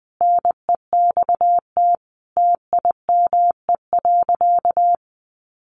Morse code, heard as a keyed tone, about 20 wpm, 700 Hz. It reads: NEXT TIME.
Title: Sentences from Top 1500 Words 20wpm